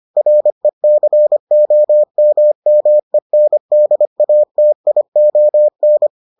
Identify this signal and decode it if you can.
Morse code, heard as a keyed tone, about 25 wpm, 600 Hz. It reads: RECOMMENDATION